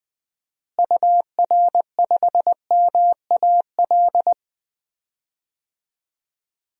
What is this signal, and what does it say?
Morse code, heard as a keyed tone, about 20 wpm, 700 Hz. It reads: UR5MAL
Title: DXCC-100 Call Signs - 6 Characters - Repeated 3x - 20wpm